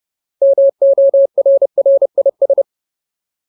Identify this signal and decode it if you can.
Morse code, heard as a keyed tone, about 30 wpm, 550 Hz. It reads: MORRIS
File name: Ultimate Word List / 30WPM Ultimate